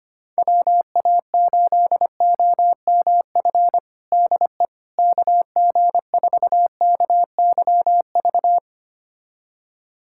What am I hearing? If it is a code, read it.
Morse code, heard as a keyed tone, about 25 wpm, 700 Hz. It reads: WA8OMF DE KG4KYV
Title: USA Callsign DE Callsign - 6 Characters - 25wpm